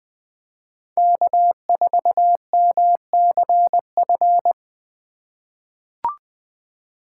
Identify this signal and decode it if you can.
Morse code, heard as a keyed tone, about 20 wpm, 700 Hz. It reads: K4MCF
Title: USA Call Signs - 5 Characters - 4x Character Spacing - 20wpm